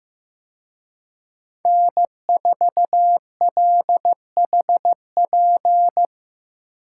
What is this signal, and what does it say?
Morse code, heard as a keyed tone, about 18 wpm, 700 Hz. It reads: N4LHP
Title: Speed Racing - USA 1x3 Call Signs 15wpm